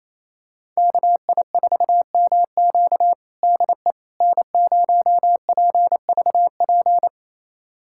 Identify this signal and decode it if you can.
Morse code, heard as a keyed tone, about 28 wpm, 700 Hz. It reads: KI4MQ DE N0PVP
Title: USA Callsign DE Callsign - 5 Characters - 28wpm